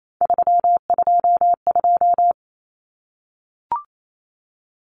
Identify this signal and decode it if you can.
Morse code, heard as a keyed tone, about 28 wpm, 700 Hz. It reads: 322
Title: Numbers - 3-Digit 28wpm